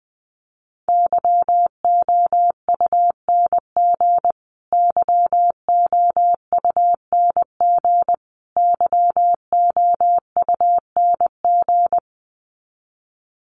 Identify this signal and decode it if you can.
Morse code, heard as a keyed tone, about 20 wpm, 700 Hz. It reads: YOUNG YOUNG YOUNG
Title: Top 500 Words Repeated 20wpm